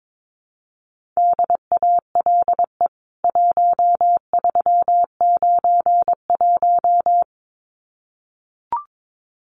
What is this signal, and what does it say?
Morse code, heard as a keyed tone, about 22 wpm, 700 Hz. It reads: DALE 1391
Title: CWT Exchange 22wpm